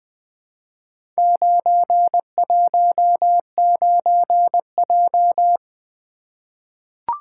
Morse code, heard as a keyed tone, about 20 wpm, 700 Hz. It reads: 919J
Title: Four Letter-Number Combinations - 20wpm